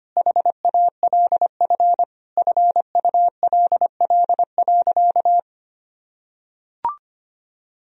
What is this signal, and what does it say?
Morse code, heard as a keyed tone, about 25 wpm, 700 Hz. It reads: HALF FULL.
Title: Sentences from Top 1000 Words - Review Part 3 - 25wpm